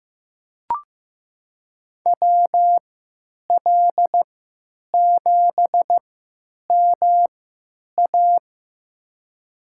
Morse code, heard as a keyed tone, about 15 wpm, spaced as for 9 wpm, 700 Hz. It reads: WL7MA
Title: USA Call Signs - 5 Characters - 3x Character Spacing - 15wpm